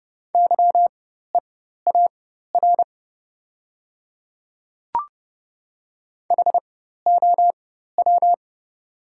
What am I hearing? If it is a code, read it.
Morse code, heard as a keyed tone, about 30 wpm, spaced as for 14 wpm, 700 Hz. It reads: YEAR HOW
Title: 4x Character Spacing - Top 200 Words - 30wpm